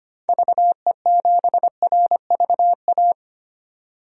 Morse code, heard as a keyed tone, about 25 wpm, 700 Hz. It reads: VE7RVA
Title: DXCC-100 Call Signs - 6 Characters - 25wpm